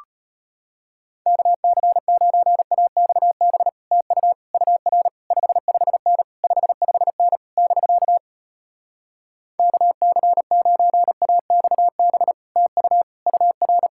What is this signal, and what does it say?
Morse code, heard as a keyed tone, about 33 wpm, 700 Hz. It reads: KC9AXB TU UR 55N 55N <BK> KC9AXB TU UR